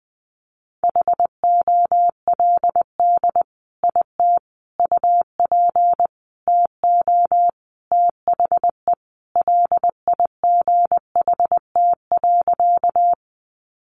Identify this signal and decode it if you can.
Morse code, heard as a keyed tone, about 19 wpm, 700 Hz. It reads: HOLD IT UP TO THE LIGHT.
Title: Sentences from Top 400 Words 20wpm